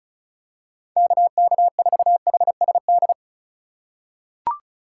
Morse code, heard as a keyed tone, about 36 wpm, 700 Hz. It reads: KK4HSD
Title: USA Call Signs - 6 Characters - 4x Character Spacing - 35wpm